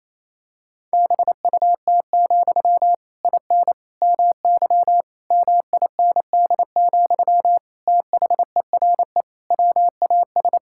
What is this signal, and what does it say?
Morse code, heard as a keyed tone, about 28 wpm, 700 Hz. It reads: BUT, IN MY MIND, THERE WAS
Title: Sentences from Top 1000 Words - Review Part 5 - 28wpm